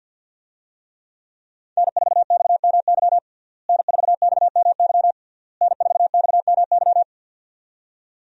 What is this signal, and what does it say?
Morse code, heard as a keyed tone, about 52 wpm, 700 Hz. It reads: N4XMY N4XMY N4XMY
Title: USA Call Signs - 5 Characters - Repeated 3x - 50wpm